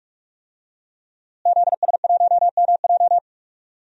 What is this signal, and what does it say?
Morse code, heard as a keyed tone, about 48 wpm, 700 Hz. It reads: ZS1MJ
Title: DXCC-100 Call Signs - 5 Characters - Repeated 2x - 45wpm